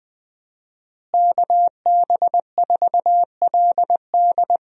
Morse code, heard as a keyed tone, about 20 wpm, 700 Hz. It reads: KB4LD
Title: USA Call Signs - 5 Characters - Repeated 2x - 20wpm